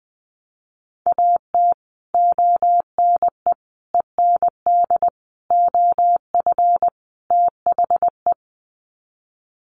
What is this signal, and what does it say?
Morse code, heard as a keyed tone, about 20 wpm, 700 Hz. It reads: AT ONE END OF THE
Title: Sets of 5 Words Encore 20wpm